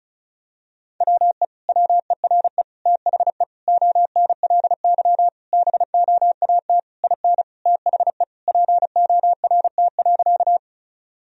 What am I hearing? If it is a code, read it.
Morse code, heard as a keyed tone, about 34 wpm, 700 Hz. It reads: WE WERE THE ONLY BOAT IN THE PORT.
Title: Sentences from Top 1000 Words - Review Part 1 - 35wpm